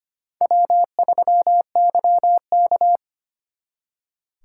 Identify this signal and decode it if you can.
Morse code, heard as a keyed tone, about 31 wpm, 700 Hz. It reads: W3YK
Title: Speed Racing - USA Call Signs 25wpm